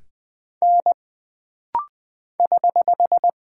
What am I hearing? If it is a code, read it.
Morse code, heard as a keyed tone, about 20 wpm, 700 Hz. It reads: N <HH>
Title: Single Character (Letter:Number:Punctuation:Prosign) - Rapid-Fire - 20wpm